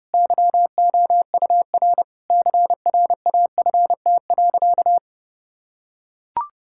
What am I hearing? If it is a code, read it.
Morse code, heard as a keyed tone, about 30 wpm, 700 Hz. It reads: YOUR CRAFT.
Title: Sentences from Top 1000 Words - Review Part 6 - 30wpm